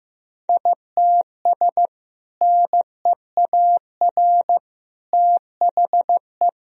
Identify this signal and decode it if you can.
Morse code, heard as a keyed tone, about 15 wpm, 700 Hz. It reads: ITS NEAR THE
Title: Sentences from Top 1000 Words - Review Part 2 - 15wpm